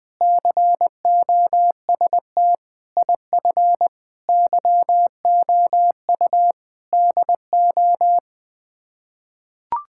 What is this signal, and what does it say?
Morse code, heard as a keyed tone, about 21 wpm, 700 Hz. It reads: COST IF YOU DO
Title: Sets of 8 Words - 20wpm